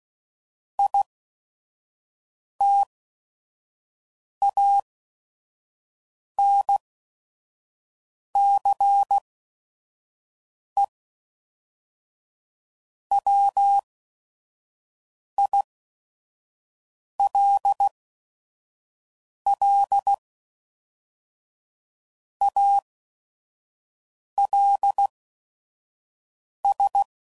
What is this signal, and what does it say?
Morse code, heard as a keyed tone, about 16 wpm, spaced as for 5 wpm, 800 Hz. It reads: ITANCE WILL ALS
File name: CW Code Groups @5WPM / Technical Subjects / XC